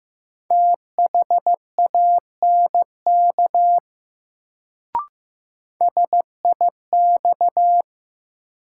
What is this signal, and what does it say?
Morse code, heard as a keyed tone, about 15 wpm, 700 Hz. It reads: THANK SIX
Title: Sentences from Top 600 Words - New Word Drill - 15wpm